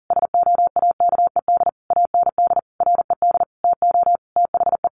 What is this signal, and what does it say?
Morse code, heard as a keyed tone, about 40 wpm, 700 Hz. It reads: SOAKED AND RED TO THE
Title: A Sound Of Thunder 40wpm